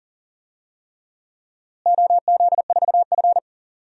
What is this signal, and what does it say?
Morse code, heard as a keyed tone, about 40 wpm, 700 Hz. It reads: OZ4F
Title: DXCC-100 Call Signs - 4 Characters - Repeated 3x - 40wpm